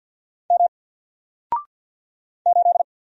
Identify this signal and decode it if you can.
Morse code, heard as a keyed tone, about 48 wpm, 700 Hz. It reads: M 8
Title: Single Letter-Number Alt 50wpm